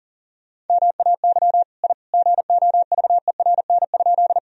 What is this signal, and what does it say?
Morse code, heard as a keyed tone, about 40 wpm, 700 Hz. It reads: MAY I GOVERN?
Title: Sentences from Top 1000 Words - Review Part 2 - 40wpm